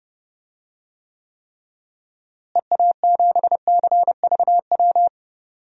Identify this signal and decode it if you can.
Morse code, heard as a keyed tone, about 32 wpm, 700 Hz. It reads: EA7CVW
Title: DXCC-100 Call Signs - 6 Characters - Repeated 3x - 30wpm